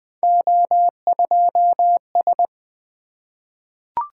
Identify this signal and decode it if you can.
Morse code, heard as a keyed tone, about 20 wpm, 700 Hz. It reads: O2S
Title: Three Letter-Number Combinations - 20wpm